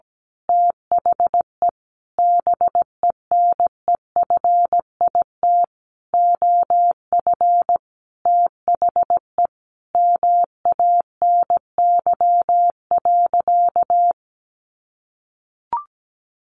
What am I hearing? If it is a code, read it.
Morse code, heard as a keyed tone, about 17 wpm, 700 Hz. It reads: THE BENEFIT OF THE MANY.
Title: Sentences from Top 1300 Words 17wpm